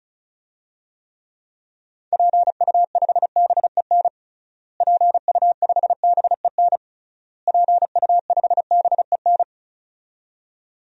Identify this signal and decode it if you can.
Morse code, heard as a keyed tone, about 34 wpm, 700 Hz. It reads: PU5BEN PU5BEN PU5BEN
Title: DXCC-100 Call Signs - 6 Characters - Repeated 3x - 35wpm